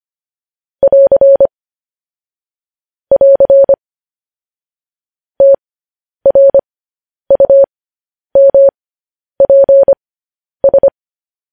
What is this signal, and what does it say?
Morse code, heard as a keyed tone, about 25 wpm, spaced as for 10 wpm, 550 Hz. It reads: <AR> <AR> TRUMPS
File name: podcast / News headlines-1x-25x10